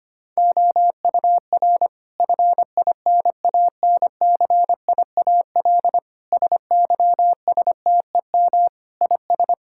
Morse code, heard as a keyed tone, about 25 wpm, 700 Hz. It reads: OUR FINANCIAL SYSTEM IS
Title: Sentences from Top 1000 Words - Review Part 5 - 25wpm